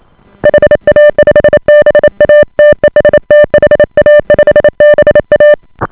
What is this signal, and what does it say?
Morse code, heard as a keyed tone, about 28 wpm, 590 Hz. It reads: HA5BA TEST HA5BA